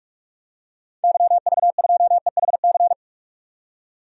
Morse code, heard as a keyed tone, about 43 wpm, 700 Hz. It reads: YV2EHC
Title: DXCC-100 Call Signs - 6 Characters - 45wpm